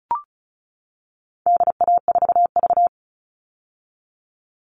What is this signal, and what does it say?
Morse code, heard as a keyed tone, about 34 wpm, 700 Hz. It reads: DA4V
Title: All Cut Numbers - 4-Digit 35wpm